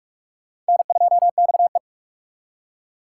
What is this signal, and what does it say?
Morse code, heard as a keyed tone, about 56 wpm, 700 Hz. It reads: N2XE
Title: Speed Racing - USA 1x2 Call Signs 45wpm